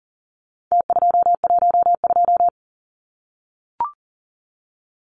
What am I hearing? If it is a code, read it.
Morse code, heard as a keyed tone, about 40 wpm, 700 Hz. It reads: T212
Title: Numbers - Contest Style - 4-Digit 40wpm